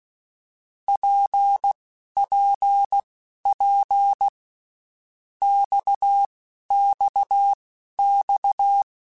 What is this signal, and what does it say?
Morse code, heard as a keyed tone, about 16 wpm, spaced as for 12 wpm, 800 Hz. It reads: PPP XXX
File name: Basic Letter Groups By7Groups / RKPX / RKPX@10